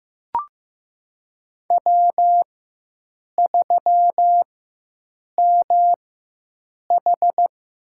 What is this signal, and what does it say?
Morse code, heard as a keyed tone, about 15 wpm, spaced as for 7 wpm, 700 Hz. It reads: W3MH
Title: USA Call Signs - 4 Characters - 4x Character Spacing - 15wpm